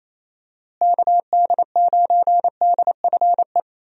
Morse code, heard as a keyed tone, about 28 wpm, 700 Hz. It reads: KD9DFE
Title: USA Call Signs - 6 Characters - Repeated 2x - 28wpm